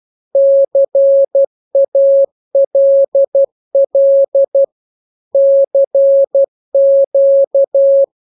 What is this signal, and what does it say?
Morse code, heard as a keyed tone, about 12 wpm, 550 Hz. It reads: CALL CQ
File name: jokes / joke 21 12 0000